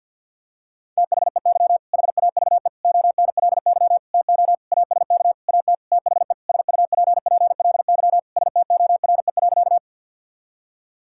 Morse code, heard as a keyed tone, about 50 wpm, 700 Hz. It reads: THEY HAVE ONLY TO ASK AT THE SUPPLY STORE.
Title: Sentences from Top 1000 Words - Review Part 5 - 50wpm